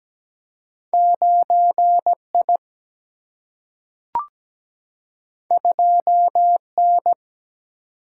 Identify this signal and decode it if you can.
Morse code, heard as a keyed tone, about 17 wpm, 700 Hz. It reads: 9I 2N
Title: Single-Letter Single-Number 17wpm